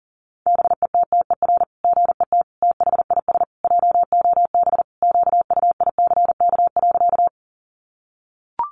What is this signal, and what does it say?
Morse code, heard as a keyed tone, about 40 wpm, 700 Hz. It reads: BETTER GET THIS JOB QUICK.
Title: Sentences from Top 600 Words 40wpm